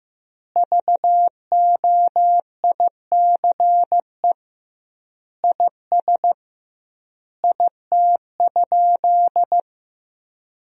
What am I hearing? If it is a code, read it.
Morse code, heard as a keyed tone, about 16 wpm, 700 Hz. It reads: VOICE IS IT?
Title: 2x Word-Spacing - Sentences from Top 500 Words - 15wpm